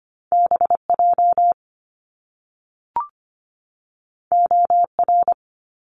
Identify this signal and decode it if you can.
Morse code, heard as a keyed tone, about 25 wpm, 700 Hz. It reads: BJ OR
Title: Top Two Letter Combinations in Words 25wpm